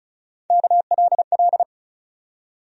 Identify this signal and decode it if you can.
Morse code, heard as a keyed tone, about 36 wpm, 700 Hz. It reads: KLL
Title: USA Call Sign Postfixes - 3 Letters 35wpm